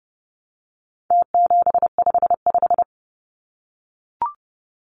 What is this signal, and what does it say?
Morse code, heard as a keyed tone, about 30 wpm, 700 Hz. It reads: T755
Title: Numbers - Contest Style - 4-Digit 30wpm